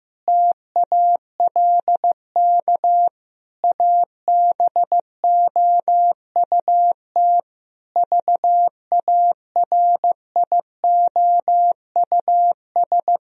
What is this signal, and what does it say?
Morse code, heard as a keyed tone, about 15 wpm, 700 Hz. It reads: TALK ABOUT VARIOUS
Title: Sentences from Top 1000 Words - Review Part 4 - 15wpm